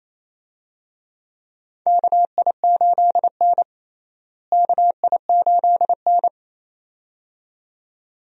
Morse code, heard as a keyed tone, about 28 wpm, 700 Hz. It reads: KI8N KI8N
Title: USA Call Signs - 4 Characters - Repeated 2x - 28wpm